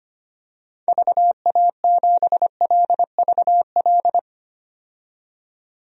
Morse code, heard as a keyed tone, about 25 wpm, 700 Hz. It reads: VA7LVL
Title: DXCC-100 Call Signs - 6 Characters - 25wpm